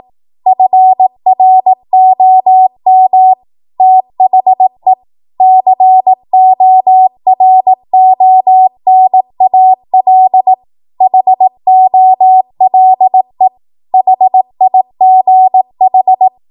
Morse code, heard as a keyed tone, about 18 wpm, 750 Hz. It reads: FROM THE CORONAL HOLE HIGH
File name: Morse / 18WPM